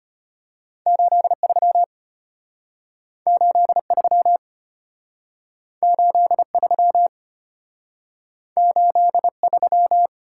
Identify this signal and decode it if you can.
Morse code, heard as a keyed tone, about 33 wpm, 700 Hz. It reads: 83 83 83 83